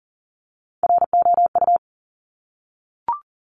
Morse code, heard as a keyed tone, about 38 wpm, 700 Hz. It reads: ROU
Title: Top Three Letter Combinations in Words 40wpm